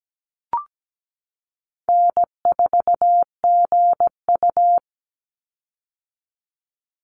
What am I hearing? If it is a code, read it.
Morse code, heard as a keyed tone, about 17 wpm, 700 Hz. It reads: N4GU
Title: Easy Call Signs - 17wpm